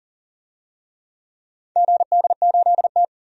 Morse code, heard as a keyed tone, about 40 wpm, 700 Hz. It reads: GD8T
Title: DXCC-100 Call Signs - 4 Characters - 40wpm